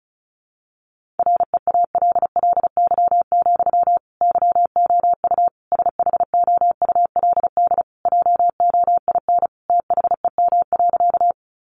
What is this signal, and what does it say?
Morse code, heard as a keyed tone, about 34 wpm, 700 Hz. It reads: REALLY, YOU SHOULD JOIN THEM.
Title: Sentences from Top 1400 Words 35wpm